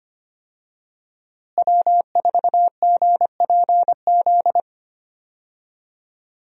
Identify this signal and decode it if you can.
Morse code, heard as a keyed tone, about 25 wpm, 700 Hz. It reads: W4GPZ
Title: USA Call Signs - 5 Characters - Repeated 3x - 25wpm